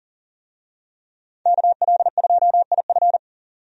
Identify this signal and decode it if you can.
Morse code, heard as a keyed tone, about 40 wpm, 700 Hz. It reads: KL2IF
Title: USA Call Signs - 5 Characters - 40wpm